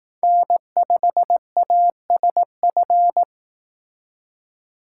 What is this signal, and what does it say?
Morse code, heard as a keyed tone, about 18 wpm, 700 Hz. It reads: N5ASF
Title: Speed Racing - USA 1x3 Call Signs 15wpm